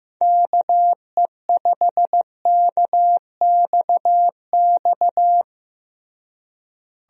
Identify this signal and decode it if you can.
Morse code, heard as a keyed tone, about 15 wpm, 700 Hz. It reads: KE5KXX
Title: USA Call Signs - 6 Characters - Repeated 2x - 15wpm